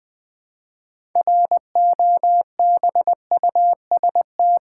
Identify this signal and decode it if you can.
Morse code, heard as a keyed tone, about 20 wpm, 700 Hz. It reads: ROBUST
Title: 6 Letter Words - First 3 Letters Given - 20wpm